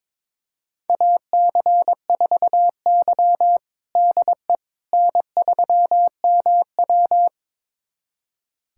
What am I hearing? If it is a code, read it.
Morse code, heard as a keyed tone, about 22 wpm, 700 Hz. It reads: AC4Y DE N3MW
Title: USA Callsign DE Callsign - 4 Characters - 22wpm